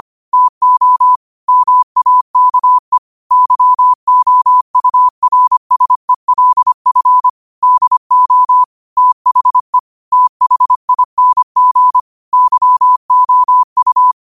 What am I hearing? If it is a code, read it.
Morse code, heard as a keyed tone, about 25 wpm, 1000 Hz. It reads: TO MAKE YOURSELF DO THE THING YOU
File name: qotdfiles / 25wpm